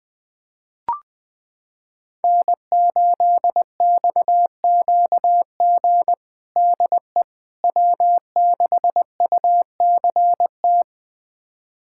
Morse code, heard as a keyed tone, about 20 wpm, 700 Hz. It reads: N8XQG DE W6UCT
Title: USA Callsign DE Callsign - 5 Characters - 20wpm